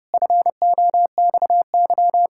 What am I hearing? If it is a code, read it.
Morse code, heard as a keyed tone, about 30 wpm, 700 Hz. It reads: FOXY
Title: Pangram Challenge v2 - 30wpm